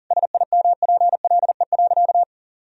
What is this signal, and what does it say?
Morse code, heard as a keyed tone, about 40 wpm, 700 Hz. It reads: SIMPLE.
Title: Sentences from Top 400 Words and Abbreviations 40wpm